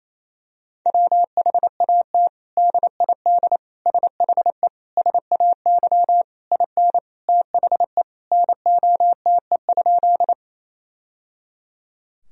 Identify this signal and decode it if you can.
Morse code, heard as a keyed tone, about 27 wpm, 700 Hz. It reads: WHAT DID SHE SAY IN THE NOTE?
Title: Sentences from Top 1000 Words - Review Part 2 - 28wpm